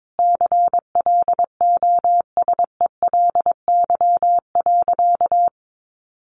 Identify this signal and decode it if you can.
Morse code, heard as a keyed tone, about 22 wpm, 700 Hz. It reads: CLOSELY.
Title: Sentences from Top 1400 Words 22wpm